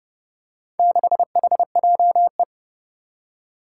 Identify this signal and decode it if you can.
Morse code, heard as a keyed tone, about 29 wpm, 700 Hz. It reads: 6HJE
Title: Four Challenging Letter-Number Combinations 30wpm